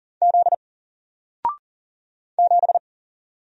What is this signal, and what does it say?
Morse code, heard as a keyed tone, about 40 wpm, 700 Hz. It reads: Z 7
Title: Single Letter-Number Alt 40wpm